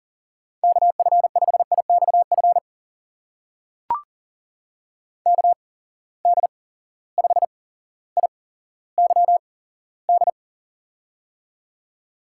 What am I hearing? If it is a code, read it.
Morse code, heard as a keyed tone, about 39 wpm, spaced as for 17 wpm, 700 Hz. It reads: KF5IXF KD5IYD
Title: USA Call Signs - 6 Characters - 8x Character Spacing - 40wpm